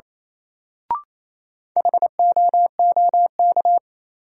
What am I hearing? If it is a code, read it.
Morse code, heard as a keyed tone, about 27 wpm, 700 Hz. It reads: HOOK
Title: Sentences from Top 1600 Words - New Word Drill Alt - 28wpm